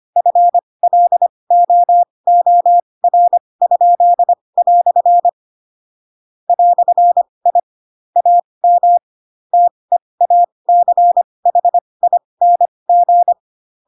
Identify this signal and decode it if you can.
Morse code, heard as a keyed tone, about 25 wpm, spaced as for 20 wpm, 700 Hz. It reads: FLOOR?" "I AM TEACHING
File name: Pinocchio / Pinocchio ch2 15 25 ik1hge